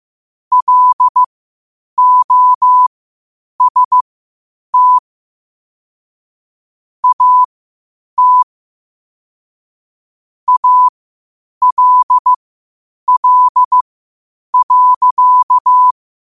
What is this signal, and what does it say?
Morse code, heard as a keyed tone, about 15 wpm, spaced as for 8 wpm, 1000 Hz. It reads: LOST AT ALL.
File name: qotdfiles / 8wpm